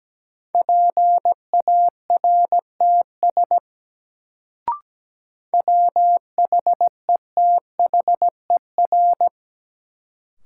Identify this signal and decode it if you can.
Morse code, heard as a keyed tone, about 17 wpm, 700 Hz. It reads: PARTS WHETHER
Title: Sets of 3 Words - New Word Drill - 17wpm